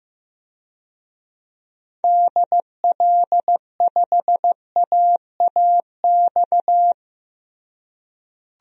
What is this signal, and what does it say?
Morse code, heard as a keyed tone, about 15 wpm, 700 Hz. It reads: DL5AAX
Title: DXCC-100 Call Signs - 6 Characters - Repeated 2x - 15wpm